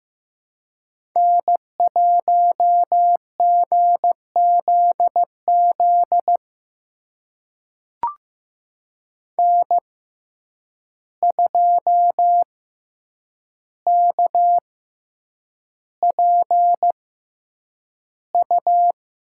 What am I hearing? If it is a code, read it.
Morse code, heard as a keyed tone, about 15 wpm, spaced as for 8 wpm, 700 Hz. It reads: N1GZZ N2KPU
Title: USA Call Signs - 5 Characters - 6x Character Spacing - 15wpm